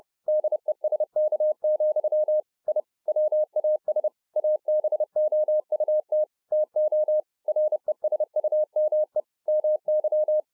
Morse code, heard as a keyed tone, about 30 wpm, 610 Hz. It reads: DESK, I WAS ABOUT TO RESUME MY